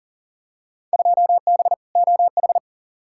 Morse code, heard as a keyed tone, about 40 wpm, 700 Hz. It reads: 2B OH
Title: Field Day Exchanges - 40wpm